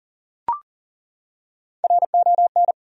Morse code, heard as a keyed tone, about 37 wpm, 700 Hz. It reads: RON
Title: USA Common Names - 40wpm